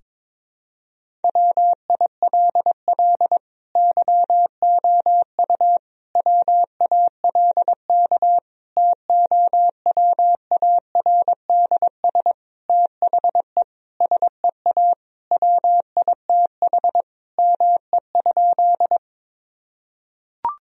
Morse code, heard as a keyed tone, about 22 wpm, 700 Hz. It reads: WILL YOU WALK TOWARDS THE SEA WITH ME?
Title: Sentences from Top 1000 Words - Review Part 2 - 22wpm